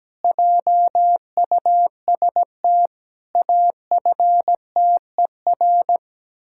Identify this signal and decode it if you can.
Morse code, heard as a keyed tone, about 17 wpm, 700 Hz. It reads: JUST AFTER
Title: Pangram Challenge v2 - 17wpm